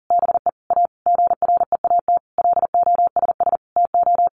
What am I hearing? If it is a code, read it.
Morse code, heard as a keyed tone, about 40 wpm, 700 Hz. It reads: BE A GREAT LOSS TO
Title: Sentences from Top 1400 Words 40wpm